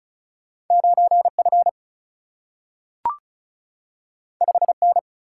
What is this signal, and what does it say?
Morse code, heard as a keyed tone, about 36 wpm, 700 Hz. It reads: 9 F 5 N
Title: Single-Letter Single-Number 35wpm